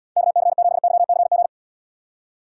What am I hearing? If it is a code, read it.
Morse code, heard as a keyed tone, about 78 wpm, 700 Hz. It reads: KC7YCG